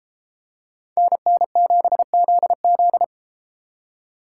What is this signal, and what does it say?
Morse code, heard as a keyed tone, about 30 wpm, 700 Hz. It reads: NN7ZZ